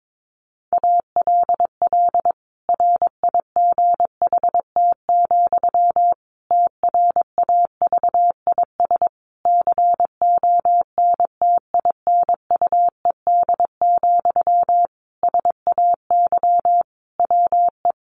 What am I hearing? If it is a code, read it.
Morse code, heard as a keyed tone, about 22 wpm, 700 Hz. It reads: ALL RIGHT, TRAVIS CONTINUED, SAY WE
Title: A Sound Of Thunder 22wpm